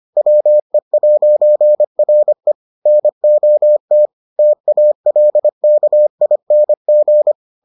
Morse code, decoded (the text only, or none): WE'RE NOT TALKING